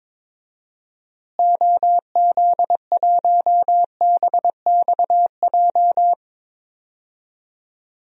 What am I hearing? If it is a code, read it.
Morse code, heard as a keyed tone, about 22 wpm, 700 Hz. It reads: OZ1BXJ
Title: DXCC-100 Call Signs - 6 Characters - Repeated 3x - 22wpm